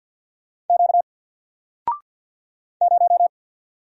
Morse code, heard as a keyed tone, about 48 wpm, 700 Hz. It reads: Q 0
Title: Single Letter-Number Alt 50wpm